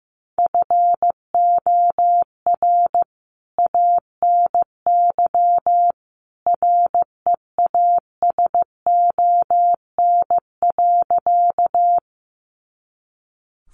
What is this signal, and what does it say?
Morse code, heard as a keyed tone, about 15 wpm, 700 Hz. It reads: FOR ANY REASON.
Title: A Sound Of Thunder 15wpm